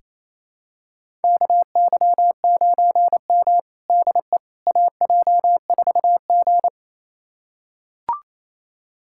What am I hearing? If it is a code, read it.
Morse code, heard as a keyed tone, about 28 wpm, 690 Hz. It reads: KY9M DE AJ4G
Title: USA Callsign DE Callsign - 4 Characters - 28wpm